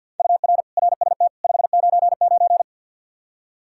Morse code, heard as a keyed tone, about 48 wpm, 700 Hz. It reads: UR RST 599
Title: Practice QSO - Exchange 1 - Beginning - 50wpm